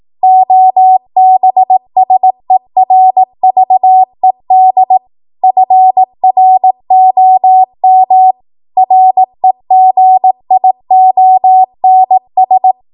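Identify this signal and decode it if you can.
Morse code, heard as a keyed tone, about 18 wpm, 750 Hz. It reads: OBSERVED FROM REGIONS
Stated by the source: 18 WPM Morse code practice from a national amateur radio association's code archive.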